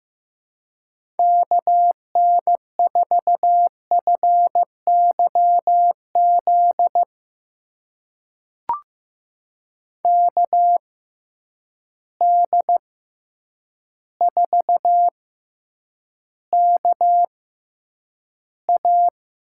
Morse code, heard as a keyed tone, about 15 wpm, spaced as for 8 wpm, 700 Hz. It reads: KN4FYZ KD4KA
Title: USA Call Signs - 6 Characters - 6x Character Spacing - 15wpm